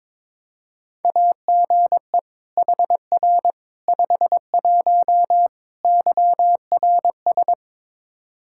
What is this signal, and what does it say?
Morse code, heard as a keyed tone, about 21 wpm, 700 Hz. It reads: AGE HR 51 YRS
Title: Practice QSO - Exchange 3 - More Info - 22wpm